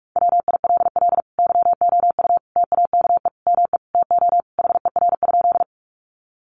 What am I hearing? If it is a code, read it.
Morse code, heard as a keyed tone, about 44 wpm, 700 Hz. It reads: WILL YOU TAKE ME TO HER?
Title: Sentences from Top 200 Words 45wpm